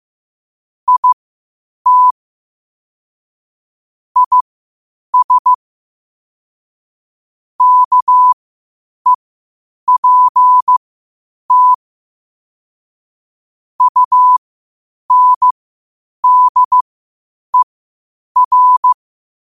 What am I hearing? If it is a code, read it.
Morse code, heard as a keyed tone, about 15 wpm, spaced as for 8 wpm, 1000 Hz. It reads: IT IS KEPT UNDER